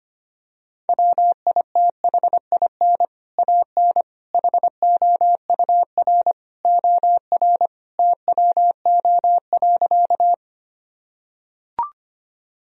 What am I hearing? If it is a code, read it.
Morse code, heard as a keyed tone, about 25 wpm, 700 Hz. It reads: WITHIN AN HOUR OR TWO.
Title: Sentences from Top 1000 Words - Review Part 4 - 25wpm